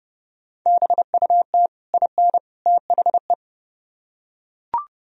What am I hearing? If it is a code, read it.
Morse code, heard as a keyed tone, about 28 wpm, 700 Hz. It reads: BUT IN THE